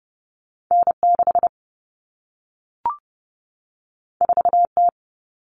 Morse code, heard as a keyed tone, about 30 wpm, 700 Hz. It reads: N 6 4 T
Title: All Cut Numbers - 2-Digit 30wpm